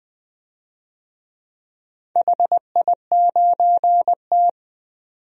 Morse code, heard as a keyed tone, about 20 wpm, 700 Hz. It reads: HI9T
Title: DXCC-100 Call Signs - 4 Characters - 20wpm